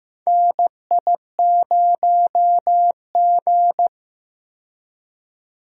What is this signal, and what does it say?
Morse code, heard as a keyed tone, about 15 wpm, 700 Hz. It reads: NI0G
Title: USA Call Signs - 4 Characters - 15wpm